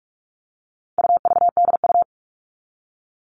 Morse code, heard as a keyed tone, about 44 wpm, 700 Hz. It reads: UVDU
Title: All Cut Numbers - 4-Digit 45wpm